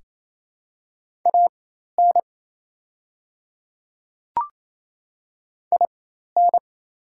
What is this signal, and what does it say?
Morse code, heard as a keyed tone, about 28 wpm, spaced as for 13 wpm, 700 Hz. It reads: AN IN
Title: 4x Character Spacing - Top 200 Words - 28wpm